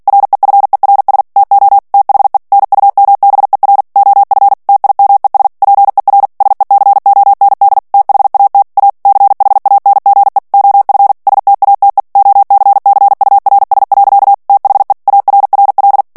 Morse code, heard as a keyed tone, about 48 wpm, 800 Hz. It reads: REFERS TO THE NUMBER OF TIMES PER SECOND THAT A CHANGE OF STATE OCCURS. THE ARRL
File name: ANGLAIS texte clair 3 vitesse 45 mots minute